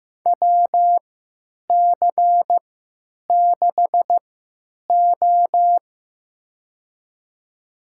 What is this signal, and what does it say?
Morse code, heard as a keyed tone, about 15 wpm, spaced as for 9 wpm, 700 Hz. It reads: WC6O
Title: USA Call Signs - 4 Characters - 3x Character Spacing - 15wpm